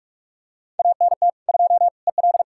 Morse code, heard as a keyed tone, about 46 wpm, 700 Hz. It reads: ANT 2 EL
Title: Practice QSO - Exchange 2 - Info - 45wpm